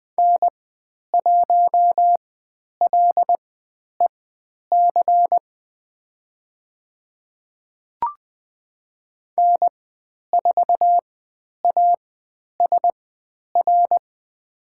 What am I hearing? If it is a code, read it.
Morse code, heard as a keyed tone, about 20 wpm, spaced as for 10 wpm, 700 Hz. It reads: N1LEC N4ASR
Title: 109 Lesson 18g - Letter m - Call Signs 20wpm